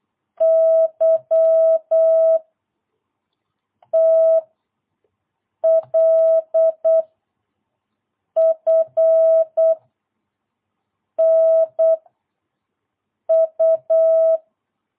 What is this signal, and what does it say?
Morse code, heard as a keyed tone, about 8 wpm, 650 Hz. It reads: Y T L F N U
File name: jh8cbh / 2022cw